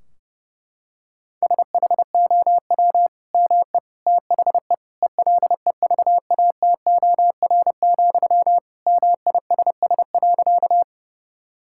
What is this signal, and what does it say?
Morse code, heard as a keyed tone, about 30 wpm, 700 Hz. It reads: SHOW ME THE ELEVATOR, MISS.
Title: Sentences from Top 1600 Words 30wpm